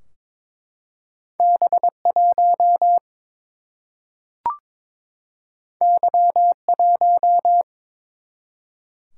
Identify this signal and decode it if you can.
Morse code, heard as a keyed tone, about 23 wpm, 700 Hz. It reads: B1 Y1
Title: Single-Letter Single-Number 22wpm